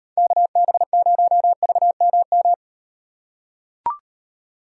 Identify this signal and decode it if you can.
Morse code, heard as a keyed tone, about 38 wpm, 700 Hz. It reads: KB0VMM